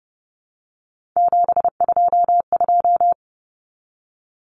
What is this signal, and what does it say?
Morse code, heard as a keyed tone, about 30 wpm, 700 Hz. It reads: 722
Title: Common Cut Numbers - 3-Digit 30wpm